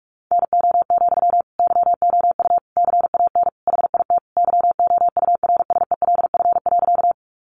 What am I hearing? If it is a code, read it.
Morse code, heard as a keyed tone, about 44 wpm, 700 Hz. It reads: NO, YOU CAN HIT YOURSELF.
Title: Sentences from Top 1200 Words 45wpm